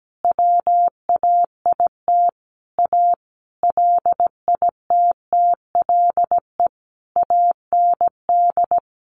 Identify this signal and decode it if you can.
Morse code, heard as a keyed tone, about 17 wpm, 700 Hz. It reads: WAIT A LITTLE AND
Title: Sentences from Top 1500 Words 17wpm